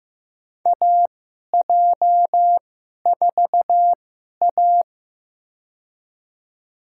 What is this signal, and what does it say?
Morse code, heard as a keyed tone, about 15 wpm, spaced as for 11 wpm, 700 Hz. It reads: AJ4A
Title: USA Call Signs - 4 Characters - 2x Character Spacing - 15wpm